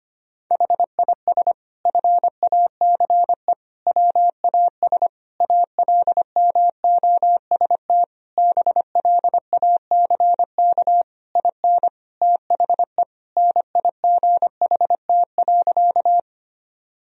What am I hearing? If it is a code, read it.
Morse code, heard as a keyed tone, about 25 wpm, 700 Hz. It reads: HIS FACE WAS ALMOST BLACK IN THE NIGHT.
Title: Sentences from Top 1000 Words - Review Part 4 - 25wpm